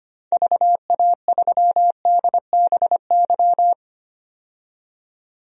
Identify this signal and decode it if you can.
Morse code, heard as a keyed tone, about 25 wpm, 700 Hz. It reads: VA3DBY
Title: DXCC-100 Call Signs - 6 Characters - 25wpm